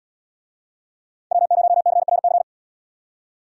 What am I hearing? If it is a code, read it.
Morse code, heard as a keyed tone, about 69 wpm, 700 Hz. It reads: W0CRC